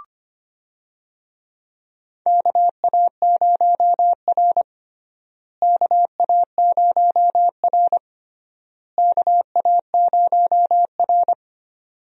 Morse code, heard as a keyed tone, about 25 wpm, 700 Hz. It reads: KA0R KA0R KA0R
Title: USA Call Signs - 4 Characters - Repeated 3x - 25wpm